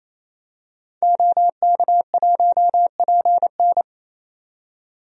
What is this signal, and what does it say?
Morse code, heard as a keyed tone, about 28 wpm, 700 Hz. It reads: OK1PN
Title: DXCC-100 Call Signs - 5 Characters - 28wpm